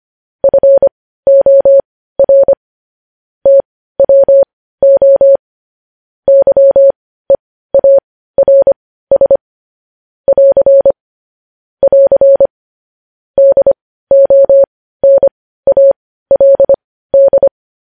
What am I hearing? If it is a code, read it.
Morse code, heard as a keyed tone, about 25 wpm, spaced as for 15 wpm, 550 Hz. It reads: FOR TWO YEARS <AR> <AR> DONALD
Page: news-headlines-in-morse-code-at-15-wpm